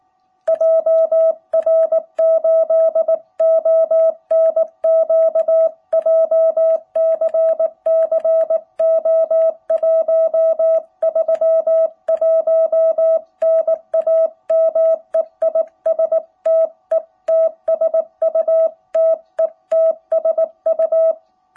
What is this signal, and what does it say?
Morse code, heard as a keyed tone, about 19 wpm, 660 Hz. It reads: JR8ONQ JCC O131 NAME IS TETSU TETSU
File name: jh8cbh / 2021cw